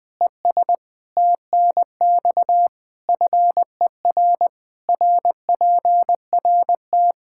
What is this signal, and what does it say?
Morse code, heard as a keyed tone, about 20 wpm, 700 Hz. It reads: ES TNX FER RPRT
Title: Practice QSO - Exchange 2 - Info - 20wpm